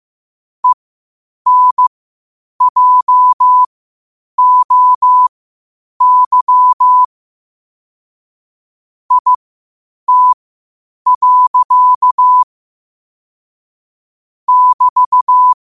Morse code, heard as a keyed tone, about 15 wpm, spaced as for 8 wpm, 1000 Hz. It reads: ENJOY IT. <BT>
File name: qotdfiles / 8wpm